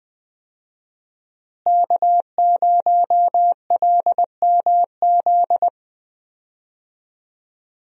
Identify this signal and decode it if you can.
Morse code, heard as a keyed tone, about 20 wpm, 700 Hz. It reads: K0LMZ
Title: USA Call Signs - 5 Characters - Repeated 3x - 20wpm